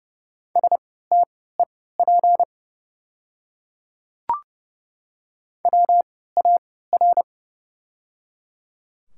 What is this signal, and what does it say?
Morse code, heard as a keyed tone, about 30 wpm, spaced as for 17 wpm, 700 Hz. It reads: STEP WAR
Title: 3x Character Spacing - Top 500 Words - 30wpm